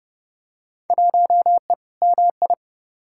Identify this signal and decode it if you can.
Morse code, heard as a keyed tone, about 32 wpm, 700 Hz. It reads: 1E MI
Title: Field Day Exchanges - 30wpm